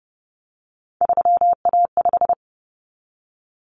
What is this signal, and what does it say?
Morse code, heard as a keyed tone, about 30 wpm, 700 Hz. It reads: 3A5
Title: Common Cut Numbers - 3-Digit 30wpm